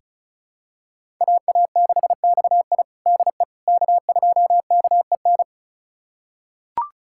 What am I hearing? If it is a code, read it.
Morse code, heard as a keyed tone, about 36 wpm, 700 Hz. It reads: AA6XI DE K2KEN
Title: USA Callsign DE Callsign - 5 Characters - 35wpm